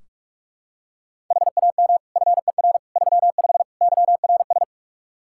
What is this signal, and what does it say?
Morse code, heard as a keyed tone, about 46 wpm, 700 Hz. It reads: HAM FER 35 YRS
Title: Practice QSO - Exchange 3 - More Info - 45wpm